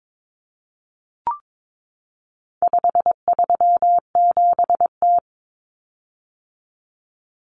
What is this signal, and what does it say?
Morse code, heard as a keyed tone, about 22 wpm, 700 Hz. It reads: 537T
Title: Common Cut Numbers - 4-Digit 22wpm